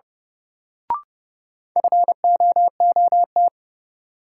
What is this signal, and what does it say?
Morse code, heard as a keyed tone, about 30 wpm, 700 Hz. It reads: FOOT
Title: Sentences from Top 500 Words - New Word Drill - 30wpm